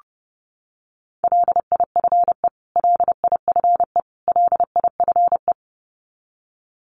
Morse code, heard as a keyed tone, about 30 wpm, 700 Hz. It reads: LIFE LIFE LIFE
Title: Top 300 Words Repeated 30wpm